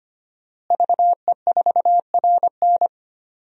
Morse code, heard as a keyed tone, about 25 wpm, 700 Hz. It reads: VE4RN
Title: DXCC-100 Call Signs - 5 Characters - Repeated 3x - 25wpm